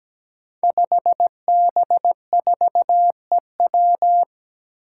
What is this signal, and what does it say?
Morse code, heard as a keyed tone, about 17 wpm, 700 Hz. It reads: 5B4EW
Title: Five Challenging Letter-Number Combinations 17wpm